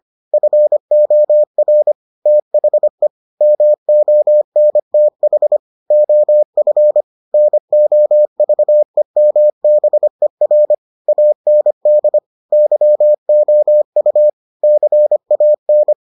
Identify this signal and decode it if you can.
Morse code, heard as a keyed tone, about 25 wpm, 600 Hz. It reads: FOR THE MONTH OF NOVEMBER AND YOU CAN